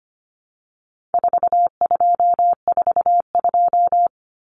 Morse code, heard as a keyed tone, about 25 wpm, 700 Hz. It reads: 4242
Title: Numbers - 4-Digit 25wpm